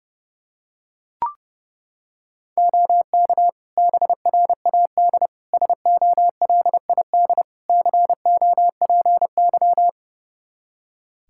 Morse code, heard as a keyed tone, about 30 wpm, 700 Hz. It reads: OK BRAD SOLID COPY
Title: K1USN SST QSO Practice 30wpm